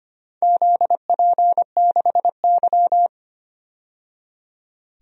Morse code, heard as a keyed tone, about 25 wpm, 700 Hz. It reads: ZP6Y
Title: DXCC-100 Call Signs - 4 Characters - 25wpm